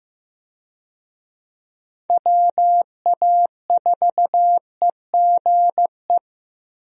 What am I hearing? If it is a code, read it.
Morse code, heard as a keyed tone, about 15 wpm, 700 Hz. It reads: WA4EGE
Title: USA Call Signs - 6 Characters - Repeated 3x - 15wpm